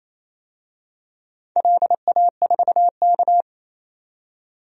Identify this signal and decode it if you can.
Morse code, heard as a keyed tone, about 28 wpm, 700 Hz. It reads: LA4K
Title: DXCC-100 Call Signs - 4 Characters - 28wpm